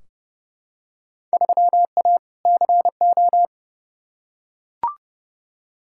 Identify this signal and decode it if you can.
Morse code, heard as a keyed tone, about 30 wpm, 700 Hz. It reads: 3A CO
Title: Field Day Exchanges Encore - 30wpm